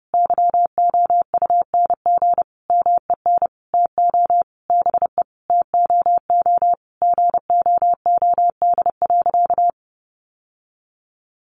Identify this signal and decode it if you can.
Morse code, heard as a keyed tone, about 30 wpm, 700 Hz. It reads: YOUNG MEN TO BE TOO GOOD.
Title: Sentences from Top 1200 Words 30wpm